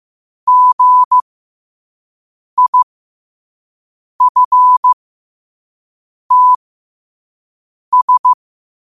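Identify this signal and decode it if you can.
Morse code, heard as a keyed tone, about 15 wpm, spaced as for 5 wpm, 1000 Hz. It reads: GIFTS
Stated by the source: Morse code sent at 5 words per minute.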